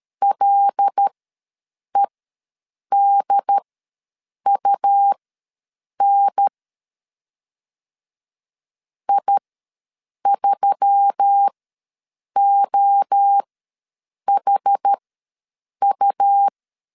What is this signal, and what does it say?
Morse code, heard as a keyed tone, about 13 wpm, spaced as for 7 wpm, 800 Hz. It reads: LEDUN I3OHU
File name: LEZIONI / 5 LBVC / 5 LB ws 12wpm 3034